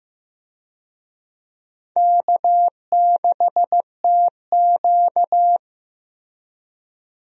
Morse code, heard as a keyed tone, about 15 wpm, 700 Hz. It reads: K6TQ
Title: USA Call Signs - 4 Characters - Repeated 3x - 15wpm